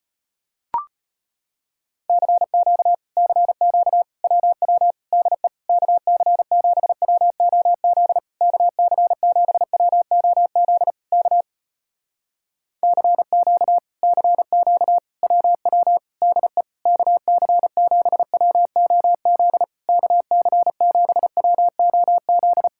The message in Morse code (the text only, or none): CQ CQ WW DE KC7WOZ KC7WOZ K CQ CQ WW DE KC7WOZ KC7WOZ